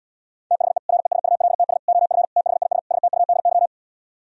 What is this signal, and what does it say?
Morse code, heard as a keyed tone, about 77 wpm, 700 Hz. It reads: THE RESULTS OF THIS STUDY